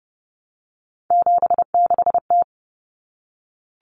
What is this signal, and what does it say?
Morse code, heard as a keyed tone, about 30 wpm, 700 Hz. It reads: 76T
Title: Common Cut Numbers - 3-Digit 30wpm